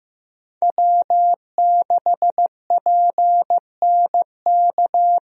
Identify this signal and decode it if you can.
Morse code, heard as a keyed tone, about 15 wpm, 700 Hz. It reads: W6PNK
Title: USA Call Signs - 5 Characters - 15wpm